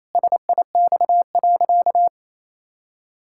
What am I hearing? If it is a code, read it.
Morse code, heard as a keyed tone, about 28 wpm, 700 Hz. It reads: SIX.
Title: Pangram Challenge v2 - 28wpm